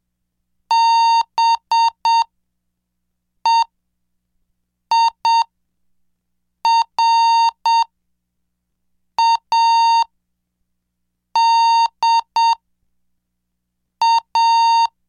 Morse code, heard as a keyed tone, about 7 wpm, spaced as for 4 wpm, 920 Hz. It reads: BEIRADA
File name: cursos / CDestudo / 11 Track 11